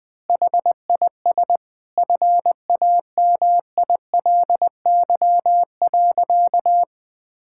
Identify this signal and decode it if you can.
Morse code, heard as a keyed tone, about 21 wpm, 700 Hz. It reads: HIS FAMILY.
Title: Sentences from Top 1000 Words - Review Part 6 - 20wpm